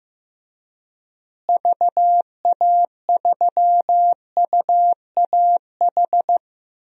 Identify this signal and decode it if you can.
Morse code, heard as a keyed tone, about 15 wpm, 700 Hz. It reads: VA3UAH
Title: DXCC-100 Call Signs - 6 Characters - Repeated 3x - 15wpm